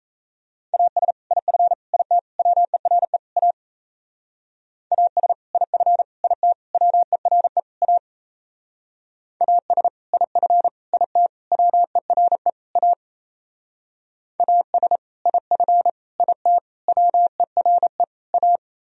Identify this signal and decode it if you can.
Morse code, heard as a keyed tone, about 33 wpm, 700 Hz. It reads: AS IF IT WERE A AS IF IT WERE A AS IF IT WERE A AS IF IT WERE A
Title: Speed-Racing - Sets of 5 Words 28wpm